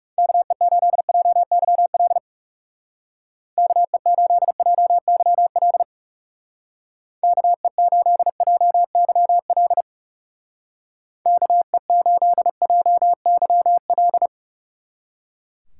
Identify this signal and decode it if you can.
Morse code, heard as a keyed tone, about 37 wpm, 700 Hz. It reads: KE8JYL KE8JYL KE8JYL KE8JYL
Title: Speed Racing - USA Call Signs 30wpm